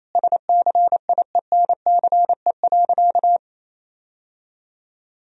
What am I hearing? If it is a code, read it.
Morse code, heard as a keyed tone, about 28 wpm, 700 Hz. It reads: SCIENCE.
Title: Sentences from Top 1000 Words - Review Part 2 - 28wpm